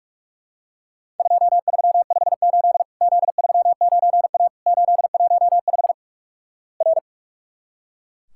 Morse code, heard as a keyed tone, about 43 wpm, 700 Hz. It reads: 2358 Z39A 815 R
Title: Worked All Europe DX Contest - QTC Exchange - 45wpm